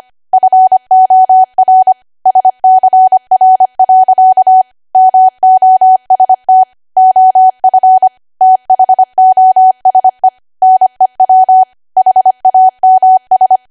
Morse code, heard as a keyed tone, about 25 wpm, 750 Hz. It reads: FOR SCR. MOST OF THOSE NEW HAMS
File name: Archive / 25 WPM / 25WPM